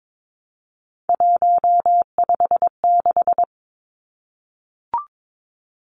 Morse code, heard as a keyed tone, about 22 wpm, 700 Hz. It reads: 156
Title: Numbers - 3-Digit 22wpm